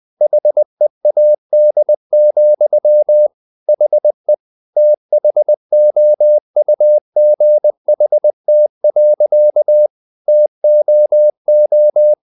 Morse code, heard as a keyed tone, about 20 wpm, 600 Hz. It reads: HEAD, HE THOUGHT. TOO
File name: the old man and the sea part 2 20wpm